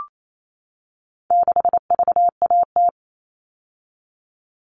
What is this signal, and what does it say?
Morse code, heard as a keyed tone, about 28 wpm, 700 Hz. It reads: 6VAT
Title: All Cut Numbers - 4-Digit 28wpm